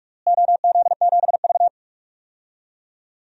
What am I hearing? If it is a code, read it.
Morse code, heard as a keyed tone, about 42 wpm, 700 Hz. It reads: OZ7V
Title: DXCC-100 Call Signs - 4 Characters - Repeated 3x - 45wpm